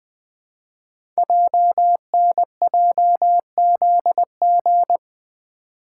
Morse code, heard as a keyed tone, about 20 wpm, 700 Hz. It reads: JNJZG
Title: Code Groups - 5 Letters - 20wpm